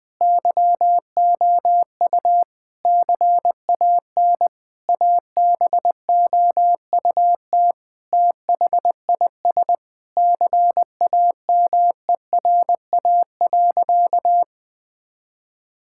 Morse code, heard as a keyed tone, about 21 wpm, 700 Hz. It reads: YOU CAN ABOUT THIS CAMERA.
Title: Sentences from Top 1000 Words - Review Part 5 - 20wpm